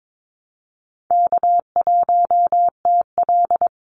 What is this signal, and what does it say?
Morse code, heard as a keyed tone, about 22 wpm, 700 Hz. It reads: K1TL
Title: Call Signs - 22wpm